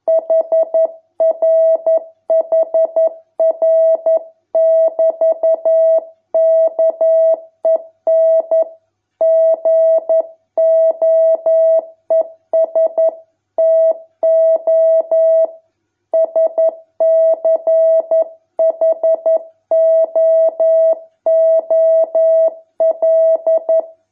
Morse code, heard as a keyed tone, about 11 wpm, 650 Hz. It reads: HRHR<BT>KEN GOES TO SCHOOL